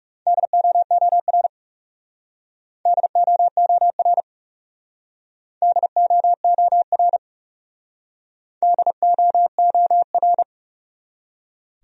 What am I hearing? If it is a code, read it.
Morse code, heard as a keyed tone, about 38 wpm, 700 Hz. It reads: DOOR DOOR DOOR DOOR
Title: Speed-Racing Top 300 Words 30wpm